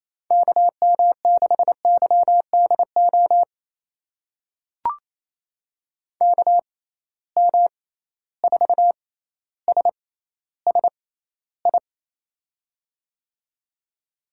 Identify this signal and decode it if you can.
Morse code, heard as a keyed tone, about 28 wpm, spaced as for 14 wpm, 700 Hz. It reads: KM6YDO KM4SSI
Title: USA Call Signs - 6 Characters - 6x Character Spacing - 28wpm